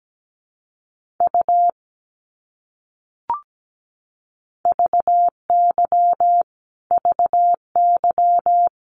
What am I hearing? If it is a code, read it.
Morse code, heard as a keyed tone, about 17 wpm, 700 Hz. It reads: U VY VY
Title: Abbreviations Repeated 3x - 17wpm